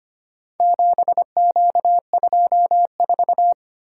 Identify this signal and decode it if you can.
Morse code, heard as a keyed tone, about 25 wpm, 700 Hz. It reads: 7Q24
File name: practice / 25wpm / Four Letter-Number Combinations - 25wpm